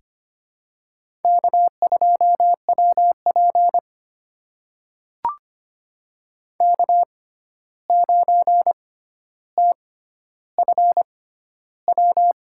Morse code, heard as a keyed tone, about 25 wpm, spaced as for 12 wpm, 700 Hz. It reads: K2WP K9TFW
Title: USA Call Signs - 6x Character Spacing - 25wpm